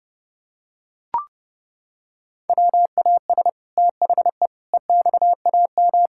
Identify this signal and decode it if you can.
Morse code, heard as a keyed tone, about 32 wpm, 700 Hz. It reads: WAS THE EXAM
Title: Sentences from Top 1000 Words - Review Part 5 - 30wpm